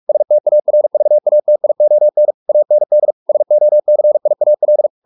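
Morse code, heard as a keyed tone, about 44 wpm, 600 Hz. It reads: STARVATION AND SOCIAL
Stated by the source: Morse practice at 45 wpm, the novel Never by Ken Follett in English, from CW med Gnister.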